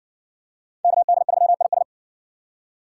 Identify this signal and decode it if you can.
Morse code, heard as a keyed tone, about 58 wpm, 700 Hz. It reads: KD3IS